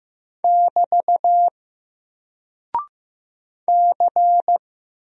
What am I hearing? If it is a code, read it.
Morse code, heard as a keyed tone, about 15 wpm, 700 Hz. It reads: <BT> C
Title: Single Character (Letter:Number:Punctuation:Prosign) Alt 15wpm